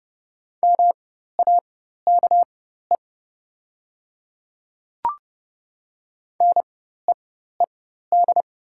Morse code, heard as a keyed tone, about 30 wpm, spaced as for 14 wpm, 690 Hz. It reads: MAKE NEED
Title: 4x Character Spacing - Top 200 Words - 30wpm